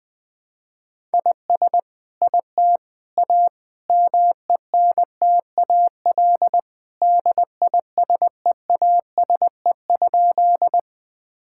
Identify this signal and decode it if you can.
Morse code, heard as a keyed tone, about 20 wpm, 700 Hz. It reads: IS IT A MENTAL DISEASE?
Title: Sentences from Top 1000 Words - Review Part 6 - 20wpm